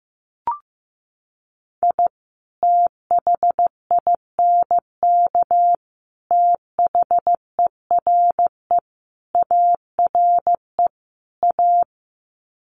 Morse code, heard as a keyed tone, about 15 wpm, 700 Hz. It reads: I THINK THERE ARE A
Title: Sets of 5 Words Encore2 15wpm